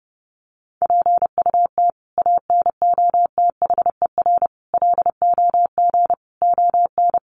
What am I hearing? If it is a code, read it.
Morse code, heard as a keyed tone, about 30 wpm, 700 Hz. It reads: PUT ANOTHER LOG ON
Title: Sentences from Top 1400 Words 30wpm